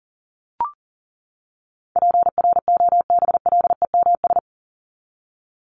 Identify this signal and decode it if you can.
Morse code, heard as a keyed tone, about 40 wpm, 700 Hz. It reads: PROBLEMS
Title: ES:S Suffix - Top 100 Words with Base Words 40wpm